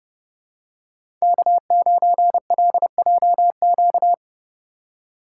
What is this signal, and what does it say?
Morse code, heard as a keyed tone, about 31 wpm, 700 Hz. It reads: K9LJQ
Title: USA Call Signs - 5 Characters - 2x Character Spacing - 30wpm